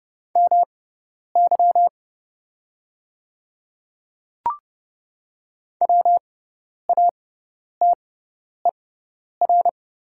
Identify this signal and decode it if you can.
Morse code, heard as a keyed tone, about 30 wpm, spaced as for 10 wpm, 700 Hz. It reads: MY WATER
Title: 6x Character Spacing - Top 100 Words - 30wpm